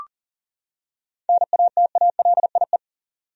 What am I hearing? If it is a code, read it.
Morse code, heard as a keyed tone, about 40 wpm, 700 Hz. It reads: NATALIE
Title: USA Common Names - 40wpm